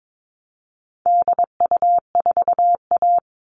Morse code, heard as a keyed tone, about 22 wpm, 700 Hz. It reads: DU4A
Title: All Cut Numbers - 4-Digit 22wpm